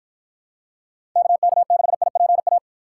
Morse code, heard as a keyed tone, about 52 wpm, 700 Hz. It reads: KK6IPA